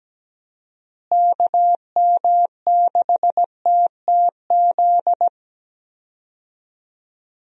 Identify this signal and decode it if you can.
Morse code, heard as a keyed tone, about 17 wpm, 700 Hz. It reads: KM6TTZ
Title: USA Call Signs - 6 Characters - Repeated 2x - 17wpm